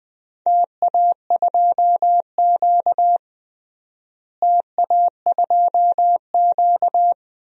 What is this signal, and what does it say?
Morse code, heard as a keyed tone, about 20 wpm, 700 Hz. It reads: TA2Q TA2Q
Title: DXCC-100 Call Signs - 4 Characters - Repeated 2x - 20wpm